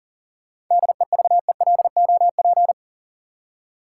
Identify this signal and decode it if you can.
Morse code, heard as a keyed tone, about 37 wpm, 700 Hz. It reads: DEVELOP
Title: Seven Letter Words - 40wpm